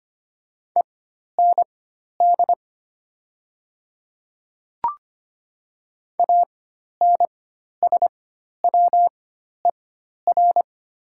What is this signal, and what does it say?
Morse code, heard as a keyed tone, about 25 wpm, spaced as for 12 wpm, 700 Hz. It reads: END ANSWER
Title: 4x Character Spacing - Top 500 Words - 25wpm